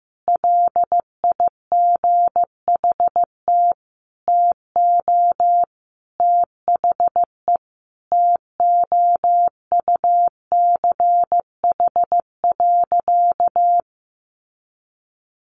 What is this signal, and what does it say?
Morse code, heard as a keyed tone, about 15 wpm, 700 Hz. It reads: LIGHT TO THE TOUCH.
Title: Sentences from Top 1500 Words 15wpm